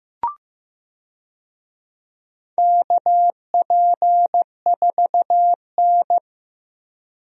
Morse code, heard as a keyed tone, about 15 wpm, 700 Hz. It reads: KP4N
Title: DXCC-100 Call Signs - 4 Characters - Repeated 3x - 15wpm